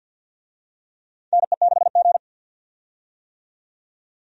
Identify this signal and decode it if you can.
Morse code, heard as a keyed tone, about 53 wpm, 700 Hz. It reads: NE6G
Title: USA Call Signs - 4 Characters - 50wpm